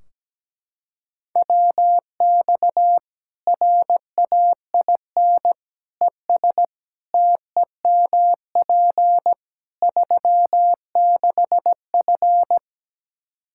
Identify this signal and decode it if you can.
Morse code, heard as a keyed tone, about 17 wpm, 700 Hz. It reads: WX RAIN ES TEMP 36F
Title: Practice QSO - Exchange 2 - Info - 17wpm